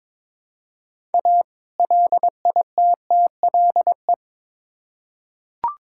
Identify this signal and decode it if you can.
Morse code, heard as a keyed tone, about 22 wpm, 700 Hz. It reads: A LITTLE
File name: Build A Phrase - 2-Word Phrases - 22wpm